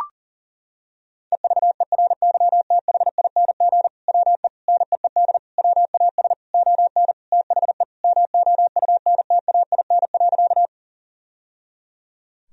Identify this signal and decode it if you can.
Morse code, heard as a keyed tone, about 40 wpm, 700 Hz. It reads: EVERYTHING WE NEED WAS ON THE MOUNTAIN.
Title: Sentences from Top 1000 Words - Review Part 1 - 40wpm